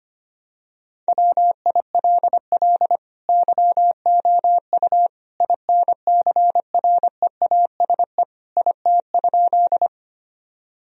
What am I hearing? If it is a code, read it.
Morse code, heard as a keyed tone, about 25 wpm, 700 Hz. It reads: WILL YOU INCREASE IT?
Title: Sentences from Top 1000 Words - Review Part 4 - 25wpm